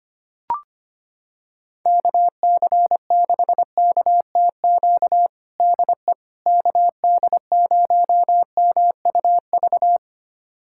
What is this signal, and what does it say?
Morse code, heard as a keyed tone, about 25 wpm, 700 Hz. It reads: KC6KTQ DE KD0MUV
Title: USA Callsign DE Callsign 25wpm